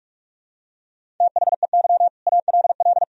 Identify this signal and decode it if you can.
Morse code, heard as a keyed tone, about 46 wpm, 700 Hz. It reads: THEY ALL
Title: Sentences from Top 500 Words and Abbreviations 45wpm